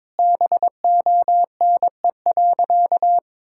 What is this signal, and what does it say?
Morse code, heard as a keyed tone, about 22 wpm, 700 Hz. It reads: BONE.
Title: Sentences from Top 1600 Words 22wpm